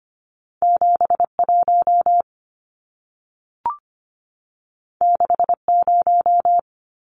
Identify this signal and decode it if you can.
Morse code, heard as a keyed tone, about 25 wpm, 700 Hz. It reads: 71 60
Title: Numbers - 2-Digit 25wpm